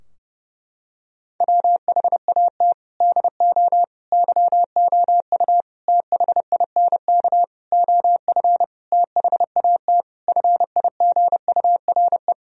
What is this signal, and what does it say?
Morse code, heard as a keyed tone, about 32 wpm, 700 Hz. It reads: WHAT DO YOU THINK OF THAT FIGURE
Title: Sentences from Top 1000 Words - Review Part 2 - 30wpm